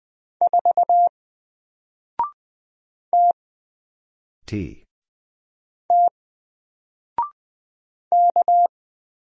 Morse code, heard as a keyed tone, about 20 wpm, spaced as for 3 wpm, 700 Hz. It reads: 4 TT K
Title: Single Letter-Number 20wpm